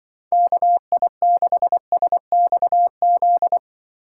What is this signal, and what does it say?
Morse code, heard as a keyed tone, about 24 wpm, 700 Hz. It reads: KI6SXZ
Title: Speed Racing - USA Call Signs 20wpm